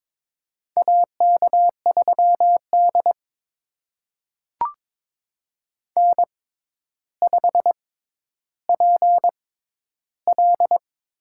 Morse code, heard as a keyed tone, about 22 wpm, spaced as for 11 wpm, 700 Hz. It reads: AK3D N5PL
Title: USA Call Signs - 4 Characters - 6x Character Spacing - 22wpm